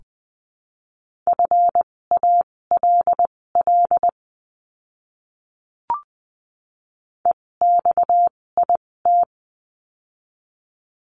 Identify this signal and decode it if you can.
Morse code, heard as a keyed tone, about 20 wpm, spaced as for 16 wpm, 700 Hz. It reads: FALL EXIT
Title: 256 Lesson 42a - Words 12-20wpm